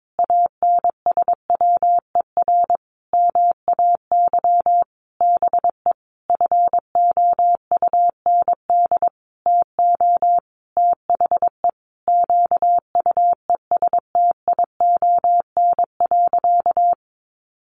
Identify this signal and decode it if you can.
Morse code, heard as a keyed tone, about 22 wpm, 700 Hz. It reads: ANSWER MAY BE FOUND TO THE QUESTION.
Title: Sentences from Top 1400 Words 22wpm